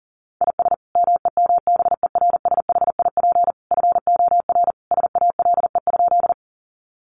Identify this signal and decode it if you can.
Morse code, heard as a keyed tone, about 40 wpm, 700 Hz. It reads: IS MEMBERSHIP FOR SALE?
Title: Sentences from Top 1400 Words 40wpm